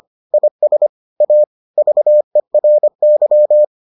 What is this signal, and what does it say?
Morse code, heard as a keyed tone, about 25 wpm, 600 Hz. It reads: IS A VERY